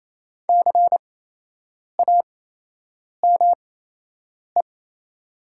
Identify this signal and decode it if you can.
Morse code, heard as a keyed tone, about 28 wpm, spaced as for 8 wpm, 700 Hz. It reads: CAME
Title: 8x Character Spacing - Top 300 Words - 28wpm